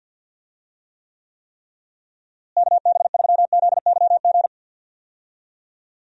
Morse code, heard as a keyed tone, about 47 wpm, 700 Hz. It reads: KB3ZYG
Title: USA Call Signs - 6 Characters - Repeated 3x - 50wpm